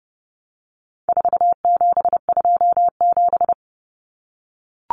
Morse code, heard as a keyed tone, about 30 wpm, 700 Hz. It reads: 4727
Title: Numbers - 4-Digit 30wpm